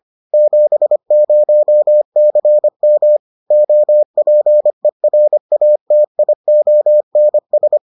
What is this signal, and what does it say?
Morse code, heard as a keyed tone, about 25 wpm, 600 Hz. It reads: 70CM OPERATIONS